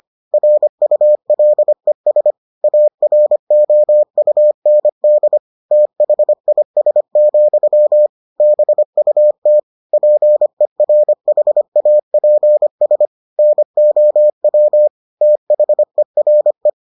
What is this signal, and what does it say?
Morse code, heard as a keyed tone, about 25 wpm, 600 Hz. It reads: RULES AROUND THIS, BUT PERHAPS NOW THERE